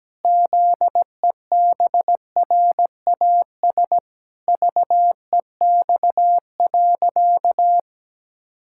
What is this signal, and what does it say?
Morse code, heard as a keyed tone, about 17 wpm, 700 Hz. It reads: ZEBRAS VEX.
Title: Pangram Challenge v2 - 17wpm